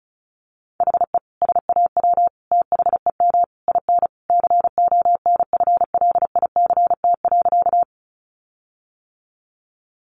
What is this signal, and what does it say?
Morse code, heard as a keyed tone, about 34 wpm, 700 Hz. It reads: HE SAW THEM IN CONFLICT.
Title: Sentences from Top 1300 Words 35wpm